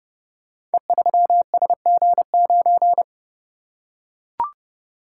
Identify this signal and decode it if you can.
Morse code, heard as a keyed tone, about 29 wpm, 700 Hz. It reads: E3SG9
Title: Code Groups - 5 Letter:Numbers - 30wpm